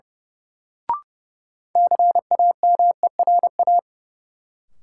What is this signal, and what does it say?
Morse code, heard as a keyed tone, about 30 wpm, 700 Hz. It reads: CAMERA
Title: Sentences from Top 1000 Words - Review Part 5 - New Word Drill - 30wpm